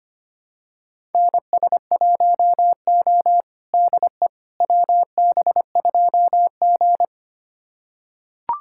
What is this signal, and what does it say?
Morse code, heard as a keyed tone, about 25 wpm, 700 Hz. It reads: NS1O DE WB2G
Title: USA Callsign DE Callsign - 4 Characters - 25wpm